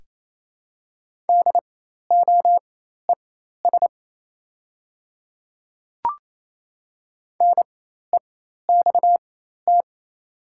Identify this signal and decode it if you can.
Morse code, heard as a keyed tone, about 28 wpm, spaced as for 13 wpm, 700 Hz. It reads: DOES NEXT
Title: 4x Character Spacing - Top 300 Words - 28wpm